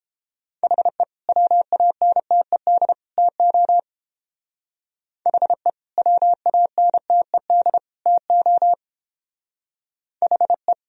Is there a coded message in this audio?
Yes: Morse code from 0.6 to 10.7 s, heard as a keyed tone, about 29 wpm, 700 Hz. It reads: HE WANTED TO HE WANTED TO HE